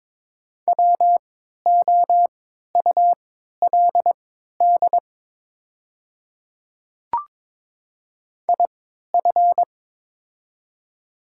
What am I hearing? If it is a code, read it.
Morse code, heard as a keyed tone, about 23 wpm, spaced as for 13 wpm, 700 Hz. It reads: WOULD IF
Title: 3x Character Spacing - Top 100 Words - 22wpm